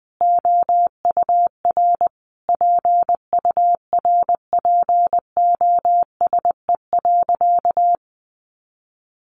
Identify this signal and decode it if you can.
Morse code, heard as a keyed tone, about 20 wpm, 700 Hz. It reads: OUR PURPOSE.
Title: Sentences from Top 800 Words 20wpm